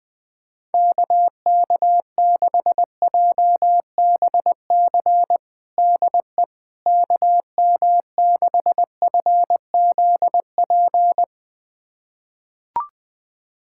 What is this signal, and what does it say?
Morse code, heard as a keyed tone, about 20 wpm, 700 Hz. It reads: KK6JBC DE KM6FZP
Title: USA Callsign DE Callsign - 6 Characters - 20wpm